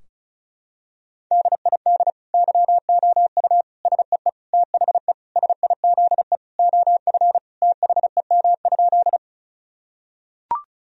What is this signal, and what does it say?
Morse code, heard as a keyed tone, about 36 wpm, 700 Hz. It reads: DID YOU SEE THE SIZE OF THEM?
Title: Sentences from Top 1000 Words - Review Part 2 - 35wpm